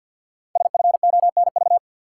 Morse code, heard as a keyed tone, about 51 wpm, 700 Hz. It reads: SFONV
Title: Code Groups - 5 Letters - 50wpm